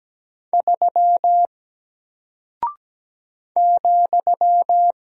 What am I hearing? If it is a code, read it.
Morse code, heard as a keyed tone, about 17 wpm, 700 Hz. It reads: 3 ,
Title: Single Character (Letter:Number:Punctuation:Prosign) 17wpm